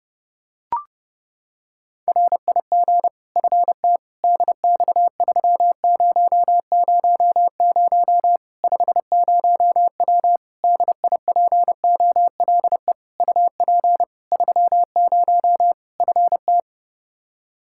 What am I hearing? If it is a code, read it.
Morse code, heard as a keyed tone, about 30 wpm, 700 Hz. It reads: RIG FT DX3000 50W DIPOLE UP 30 FT
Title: Practice QSO - Exchange 2 - Info - 30wpm